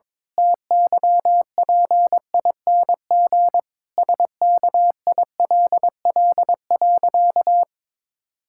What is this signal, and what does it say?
Morse code, heard as a keyed tone, about 21 wpm, 700 Hz. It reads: TYPING SKILL.
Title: Pangram Challenge v2 - 22wpm